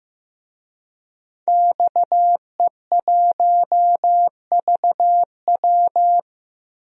Morse code, heard as a keyed tone, about 16 wpm, 700 Hz. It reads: XE1VW
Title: DXCC-100 Call Signs - 5 Characters - 15wpm